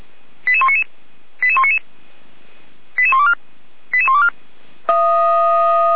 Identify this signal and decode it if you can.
Five-tone selective call (ZVEI), heard as quick runs of five tones, tones of about 70 ms. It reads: 88199 88199 88124 88124
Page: audio.f85360.feuerwehr-orlamuende.org